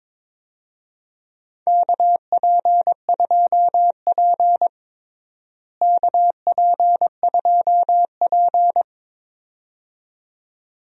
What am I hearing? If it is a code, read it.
Morse code, heard as a keyed tone, about 22 wpm, 700 Hz. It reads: KP2P KP2P
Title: DXCC-100 Call Signs - 4 Characters - Repeated 2x - 22wpm